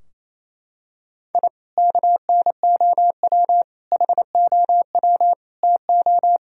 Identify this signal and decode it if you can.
Morse code, heard as a keyed tone, about 27 wpm, 700 Hz. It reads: I KNOW HOW TO
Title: Sentences from Top 1000 Words - Review Part 2 - 28wpm